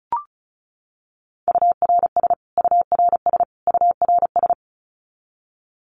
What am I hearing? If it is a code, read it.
Morse code, heard as a keyed tone, about 34 wpm, 700 Hz. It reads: URS URS URS
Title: Abbreviations Repeated 3x - 35wpm